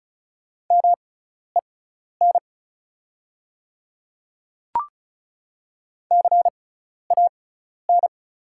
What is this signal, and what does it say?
Morse code, heard as a keyed tone, about 36 wpm, spaced as for 12 wpm, 690 Hz. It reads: MEN CAN
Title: 6x Character Spacing - Top 200 Words - 35wpm